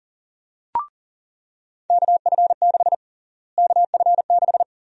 Morse code, heard as a keyed tone, about 40 wpm, 700 Hz. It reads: KF6 KF6
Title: USA Call Sign Prefixes - Repeated 2x - 40wpm